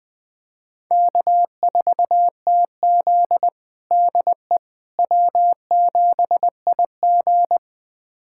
Morse code, heard as a keyed tone, about 20 wpm, 700 Hz. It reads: K4TZ DE W7IG
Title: USA Callsign DE Callsign - 4 Characters - 20wpm